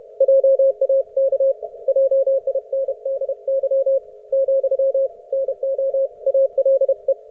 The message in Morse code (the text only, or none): JAK JINDY , NO ALE